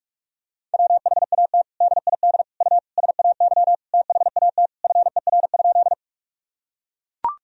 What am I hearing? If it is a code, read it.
Morse code, heard as a keyed tone, about 41 wpm, 700 Hz. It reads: WHAT DID U SAY THAT FER?
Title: Sentences from Top 200 Words and Abbreviations 45wpm